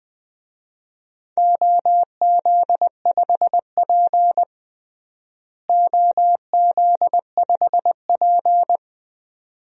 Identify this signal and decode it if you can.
Morse code, heard as a keyed tone, about 20 wpm, 700 Hz. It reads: OZ5P OZ5P
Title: DXCC-100 Call Signs - 4 Characters - Repeated 2x - 20wpm